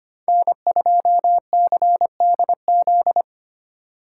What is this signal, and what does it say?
Morse code, heard as a keyed tone, about 31 wpm, 700 Hz. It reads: N2CDZ
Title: Speed Racing - USA 1x3 Call Signs 25wpm